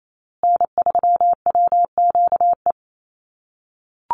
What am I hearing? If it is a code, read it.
Morse code, heard as a keyed tone, about 28 wpm, 700 Hz. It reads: N3WQE
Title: USA Call Signs Repeated 2x 28wpm